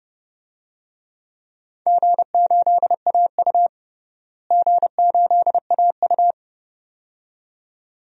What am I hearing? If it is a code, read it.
Morse code, heard as a keyed tone, about 30 wpm, 700 Hz. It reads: G8AU G8AU
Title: DXCC-100 Call Signs - 4 Characters - Repeated 2x - 30wpm